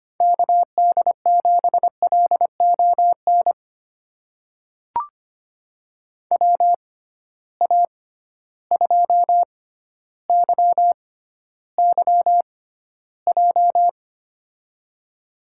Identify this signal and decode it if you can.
Morse code, heard as a keyed tone, about 25 wpm, spaced as for 13 wpm, 700 Hz. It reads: KD7LON WA2YYJ
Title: USA Call Signs - 6 Characters - 6x Character Spacing - 25wpm